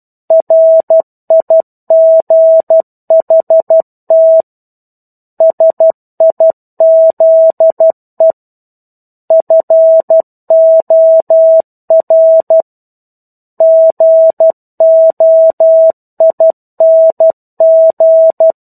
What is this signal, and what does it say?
Morse code, heard as a keyed tone, about 12 wpm, 650 Hz. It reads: RIGHT SIZE FOR GOING
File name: 01-Chapter-01 12 WPM